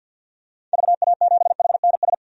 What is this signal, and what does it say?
Morse code, heard as a keyed tone, about 52 wpm, 700 Hz. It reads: VA7HNS
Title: DXCC-100 Call Signs - 6 Characters - 50wpm